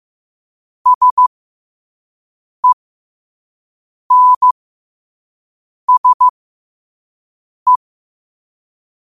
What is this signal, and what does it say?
Morse code, heard as a keyed tone, about 15 wpm, spaced as for 5 wpm, 1000 Hz. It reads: SENSE
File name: qotdfiles / 5wpm